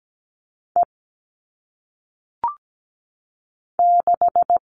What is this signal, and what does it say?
Morse code, heard as a keyed tone, about 17 wpm, 700 Hz. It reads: E 6
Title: All Cut Numbers - 1-Digit 17wpm